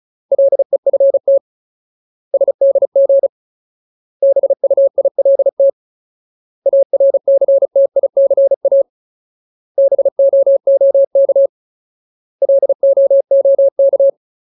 Morse code, heard as a keyed tone, about 34 wpm, 550 Hz. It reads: LEFT SDG BUILT ARCTICA BOOK LOOK
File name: Ultimate Word List / 35WPM Ultimate